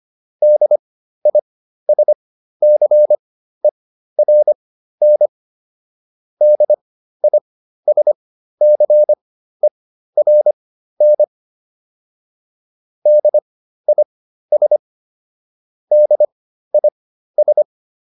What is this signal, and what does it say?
Morse code, heard as a keyed tone, about 25 wpm, spaced as for 13 wpm, 600 Hz. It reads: DISCERN DISCERN DIS DIS
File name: DIS7 13